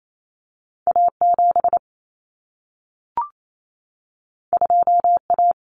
Morse code, heard as a keyed tone, about 28 wpm, 700 Hz. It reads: A7 2A
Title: Common Cut Numbers - 2-Digit 28wpm